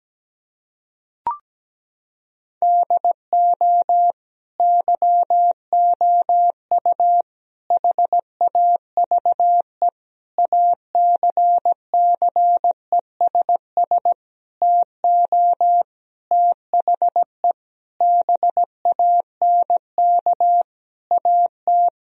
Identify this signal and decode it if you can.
Morse code, heard as a keyed tone, about 17 wpm, 700 Hz. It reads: DO YOU HAVE ACCESS TO THE BANK AT
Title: Sentences from Top 1000 Words - Review Part 4 - 17wpm